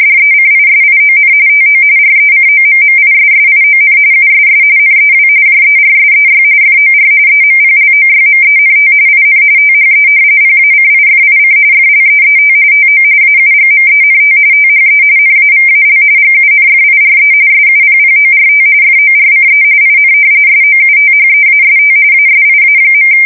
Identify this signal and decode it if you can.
RTTY, about 45.45 baud. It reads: DECISION TO RECOMMISSION USS NEW JERSEY FOR EMPLOYMENT IN THE PACIFIC FLEET TO AUGMENT THE NAVAL GUNFIRE SUPPORT FORCE IN SOUTHEAST ASIA.
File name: fsk170 / battleshipnj